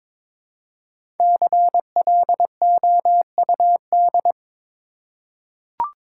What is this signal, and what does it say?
Morse code, heard as a keyed tone, about 22 wpm, 700 Hz. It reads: CLOUD
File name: 5 Letter Words - First 3 Letters Given - 22wpm